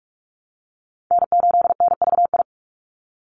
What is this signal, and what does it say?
Morse code, heard as a keyed tone, about 44 wpm, 700 Hz. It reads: N8NVI
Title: USA Call Signs 45wpm